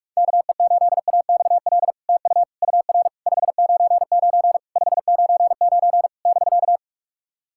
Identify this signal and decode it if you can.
Morse code, heard as a keyed tone, about 46 wpm, 700 Hz. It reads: KE8AXL TU UR 599 599 <BK>
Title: SOTA QSO Practice 45wpm